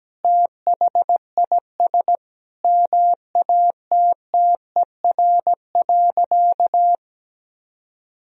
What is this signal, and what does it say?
Morse code, heard as a keyed tone, about 17 wpm, 700 Hz. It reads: THIS MATTER.
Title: Sentences from Top 1000 Words - Review Part 2 - 17wpm